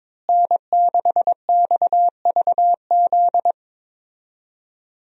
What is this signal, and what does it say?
Morse code, heard as a keyed tone, about 22 wpm, 700 Hz. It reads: N6XVZ
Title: USA Call Signs - 5 Characters - Repeated 3x - 22wpm